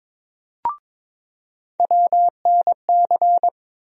Morse code, heard as a keyed tone, about 21 wpm, 700 Hz. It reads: WNC
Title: USA Call Sign Postfixes - 3 Letters 22wpm